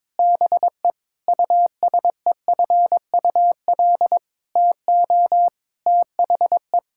Morse code, heard as a keyed tone, about 22 wpm, 700 Hz. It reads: BE USEFUL TO THE
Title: Sentences from Top 1000 Words - Review Part 6 - 22wpm